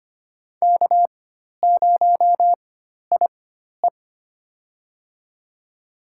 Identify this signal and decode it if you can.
Morse code, heard as a keyed tone, about 25 wpm, spaced as for 12 wpm, 700 Hz. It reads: K0IE
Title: USA Call Signs - 4 Characters - 4x Character Spacing - 25wpm